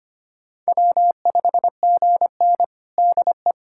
Morse code, heard as a keyed tone, about 25 wpm, 700 Hz. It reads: W5GN DE
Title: USA Callsign DE Callsign - 4 Characters - 25wpm